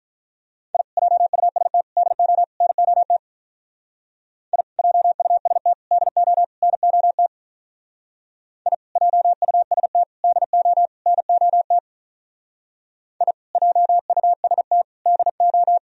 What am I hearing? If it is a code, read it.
Morse code, heard as a keyed tone, about 43 wpm, 700 Hz. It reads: I JUST DO NOT I JUST DO NOT I JUST DO NOT I JUST DO
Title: Speed-Racing - Sets of 4 Words 35wpm